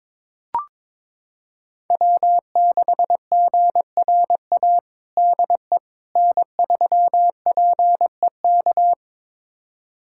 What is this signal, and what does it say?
Morse code, heard as a keyed tone, about 21 wpm, 700 Hz. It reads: W6GRA DE N3PEK
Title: USA Callsign DE Callsign 22wpm